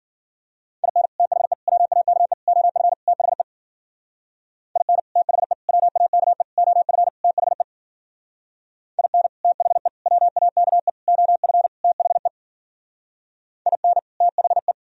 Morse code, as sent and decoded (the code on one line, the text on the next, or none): .. -. / - .... / .-- .- -.- . / --- ..-. / - .... / .. -. / - .... / .-- .- -.- . / --- ..-. / - .... / .. -. / - .... / .-- .- -.- . / --- ..-. / - .... / .. -. / - .... .
IN THE WAKE OF THE IN THE WAKE OF THE IN THE WAKE OF THE IN THE